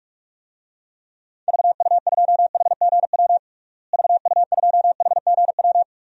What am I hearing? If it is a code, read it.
Morse code, heard as a keyed tone, about 47 wpm, 700 Hz. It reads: VU2HGW VU2HGW
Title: DXCC-100 Call Signs - 6 Characters - Repeated 2x - 45wpm